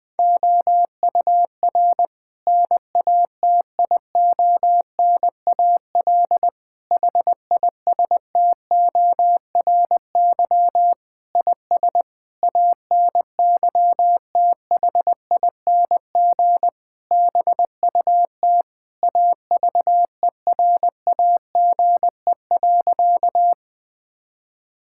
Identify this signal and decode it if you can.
Morse code, heard as a keyed tone, about 21 wpm, 700 Hz. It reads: OUR NATIONAL HISTORY IS ANYTHING BUT AVERAGE.
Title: Sentences from Top 1000 Words - Review Part 6 - 20wpm